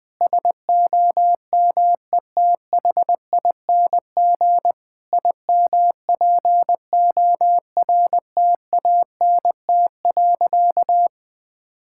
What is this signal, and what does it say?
Morse code, heard as a keyed tone, about 20 wpm, 700 Hz. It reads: SOMETHING IMPORTANT.
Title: Sentences from Top 1000 Words - Review Part 6 - 20wpm